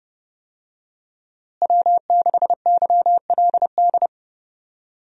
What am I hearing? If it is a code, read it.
Morse code, heard as a keyed tone, about 32 wpm, 700 Hz. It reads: W6YLD
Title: USA Call Signs - 5 Characters - Repeated 3x - 30wpm